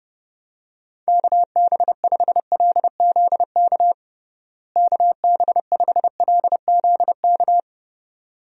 Morse code, heard as a keyed tone, about 30 wpm, 700 Hz. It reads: KB5LZK KB5LZK
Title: USA Call Signs - 6 Characters - Repeated 2x - 30wpm